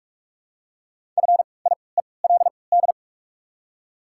Morse code, heard as a keyed tone, about 43 wpm, spaced as for 25 wpm, 700 Hz. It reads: FIELD
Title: 3x Character Spacing - Top 500 Words - 45wpm